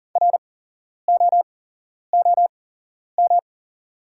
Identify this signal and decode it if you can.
Morse code, heard as a keyed tone, about 40 wpm, spaced as for 11 wpm, 690 Hz. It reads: ROOM
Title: 8x Character Spacing - Top 300 Words - 40wpm